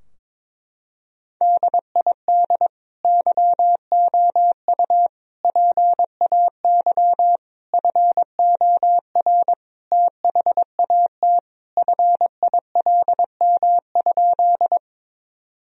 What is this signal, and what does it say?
Morse code, heard as a keyed tone, about 22 wpm, 700 Hz. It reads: DID YOU PAY FOR THAT FILM?
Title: Sentences from Top 1000 Words - Review Part 3 - 22wpm